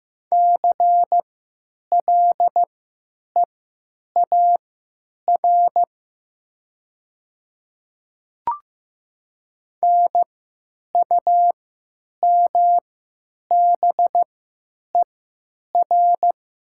Morse code, heard as a keyed tone, about 15 wpm, spaced as for 9 wpm, 700 Hz. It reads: CLEAR NUMBER
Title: 3x Character Spacing - Top 500 Words - 15wpm